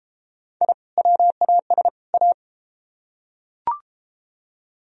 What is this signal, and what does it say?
Morse code, heard as a keyed tone, about 30 wpm, 700 Hz. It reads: I WAS A